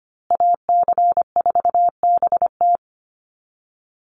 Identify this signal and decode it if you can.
Morse code, heard as a keyed tone, about 25 wpm, 700 Hz. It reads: AC4BT
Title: Call Signs Repeated 3x - 25wpm